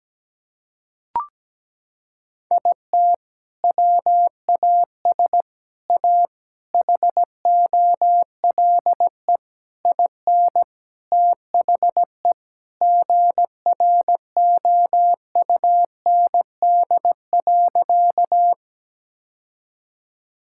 Morse code, heard as a keyed tone, about 17 wpm, 700 Hz. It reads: IT WAS A HOLE IN THE GROUND.
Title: Sentences from Top 1600 Words 17wpm